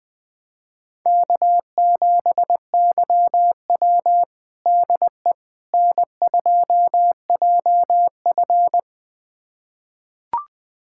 Morse code, heard as a keyed tone, about 20 wpm, 700 Hz. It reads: K7YW DE N2JF
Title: USA Callsign DE Callsign - 4 Characters - 20wpm